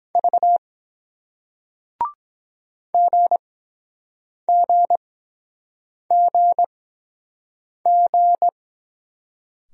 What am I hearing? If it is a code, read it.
Morse code, heard as a keyed tone, about 23 wpm, spaced as for 7 wpm, 700 Hz. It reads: V GGGG